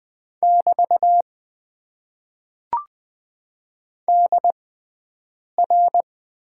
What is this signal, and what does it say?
Morse code, heard as a keyed tone, about 20 wpm, spaced as for 7 wpm, 700 Hz. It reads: <BT> DR
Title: Top 90 QSO Elements - 6x Character Spacing - 20wpm